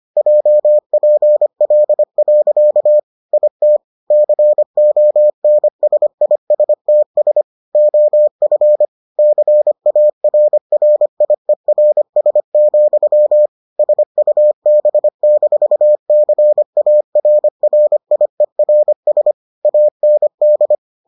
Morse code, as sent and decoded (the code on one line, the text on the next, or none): .--- .--. .-.. .-.-.- / .. - / -.-. --- -. ... .. ... - ... / --- ..-. / -.-. .- .-. .-. .. .-. ... --..-- / ... ..- -... -....- -.-. .- .-. .-. .. .-. ... / .- -. -..
JPL. IT CONSISTS OF CARRIERS, SUB-CARRIERS AND